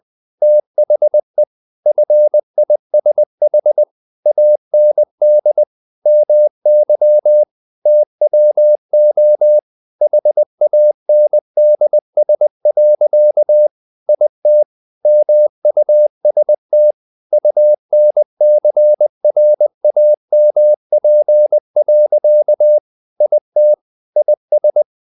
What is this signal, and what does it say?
Morse code, heard as a keyed tone, about 20 wpm, 600 Hz. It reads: THE FISH AND MY TWO HANDS. IT MUST UNCRAMP. IT IS
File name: the old man and the sea / the old man and the sea part 1 20wpm